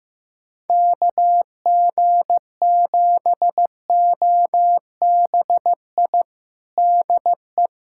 Morse code, heard as a keyed tone, about 15 wpm, 700 Hz. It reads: KG7OBI DE
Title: USA Callsign DE Callsign - 6 Characters - 15wpm